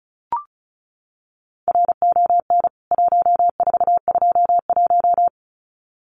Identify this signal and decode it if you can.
Morse code, heard as a keyed tone, about 34 wpm, 700 Hz. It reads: RON 1421
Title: CWT Exchange 35wpm